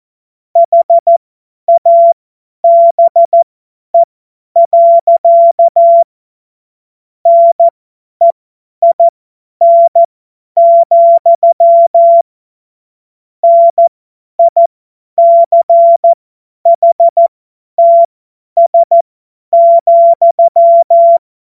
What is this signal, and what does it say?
Morse code, heard as a keyed tone, about 14 wpm, spaced as for 10 wpm, 680 Hz. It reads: HABE. NEIN, NICHTS,
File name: Morsekurs / Witze / Witz 07.November 2023 50 F70